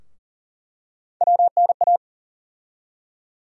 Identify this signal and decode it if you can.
Morse code, heard as a keyed tone, about 40 wpm, 700 Hz. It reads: WNA
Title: USA Call Sign Postfixes - 3 Letters 40wpm